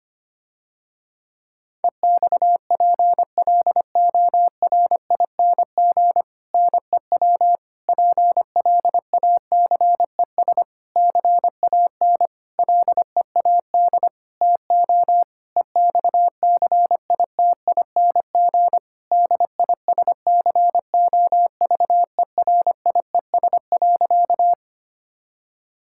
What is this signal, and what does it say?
Morse code, heard as a keyed tone, about 25 wpm, 700 Hz. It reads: EXPLORING NEW PLACES CAN LEAD TO EXCITING DISCOVERIES.
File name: practice / 25wpm / Difficult Sentences with Hints - 25wpm